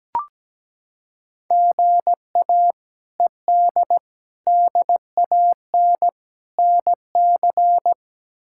Morse code, heard as a keyed tone, about 17 wpm, 700 Hz. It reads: GA ED DAN NC
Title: K1USN SST QSO Practice - 17wpm